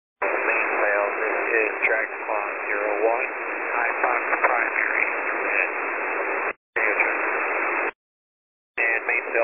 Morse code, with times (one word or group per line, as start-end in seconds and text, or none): none